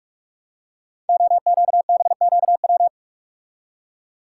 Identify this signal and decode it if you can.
Morse code, heard as a keyed tone, about 44 wpm, 690 Hz. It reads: OQBQW
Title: Code Groups - 5 Letters - 45wpm